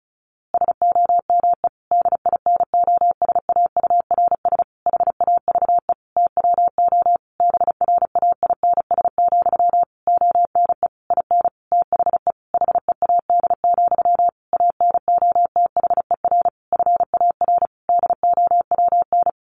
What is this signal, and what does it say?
Morse code, heard as a keyed tone, about 34 wpm, 700 Hz. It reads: SOME DINOSAURS HAVE TWO BRAINS, ONE IN THE HEAD, ANOTHER FAR DOWN
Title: A Sound Of Thunder 35wpm